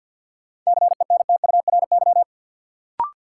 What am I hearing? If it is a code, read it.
Morse code, heard as a keyed tone, about 50 wpm, 700 Hz. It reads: CENTURY
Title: Sets of 5 Words - New Word Drill - 50wpm